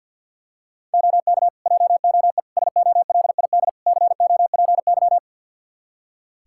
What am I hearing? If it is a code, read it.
Morse code, heard as a keyed tone, about 47 wpm, 700 Hz. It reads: OK JOE SOLID COPY
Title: Practice QSO - Exchange 3 - More Info - 50wpm